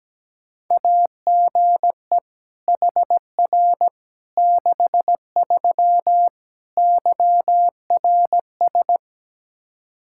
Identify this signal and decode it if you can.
Morse code, heard as a keyed tone, about 17 wpm, 700 Hz. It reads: AGE HR 63 YRS
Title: Practice QSO - Exchange 3 - More Info - 17wpm